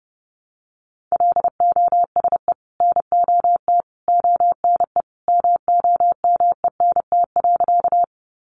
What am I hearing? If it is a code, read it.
Morse code, heard as a keyed tone, about 30 wpm, 700 Hz. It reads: LOSE NOT ONE MOMENT.
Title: Sentences from Top 1100 Words 30wpm